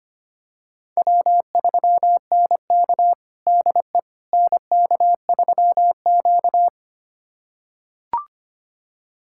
Morse code, heard as a keyed tone, about 25 wpm, 700 Hz. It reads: W3NK DE NK3Q
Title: USA Callsign DE Callsign - 4 Characters - 25wpm